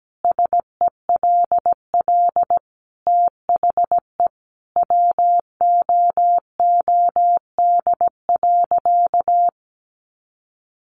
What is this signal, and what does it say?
Morse code, heard as a keyed tone, about 17 wpm, 700 Hz. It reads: SELL THE WOOD.
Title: Sentences from Top 1100 Words 17wpm